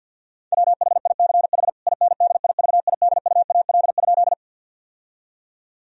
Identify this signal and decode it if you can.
Morse code, heard as a keyed tone, about 52 wpm, 700 Hz. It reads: WHICH INDIVIDUAL?
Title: Sentences from Top 1000 Words - Review Part 4 - 50wpm